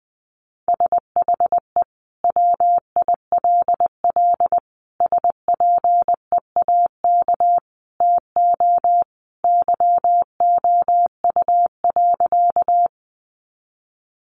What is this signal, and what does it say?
Morse code, heard as a keyed tone, about 20 wpm, 700 Hz. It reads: SHE WILL SPEAK TO YOU.
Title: Sentences from Top 1100 Words 20wpm